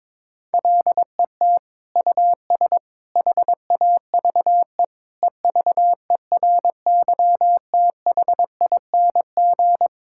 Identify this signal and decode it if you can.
Morse code, heard as a keyed tone, about 22 wpm, 700 Hz. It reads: LET US HAVE EVERYTHING
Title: Sentences from Top 1000 Words - Review Part 6 - 22wpm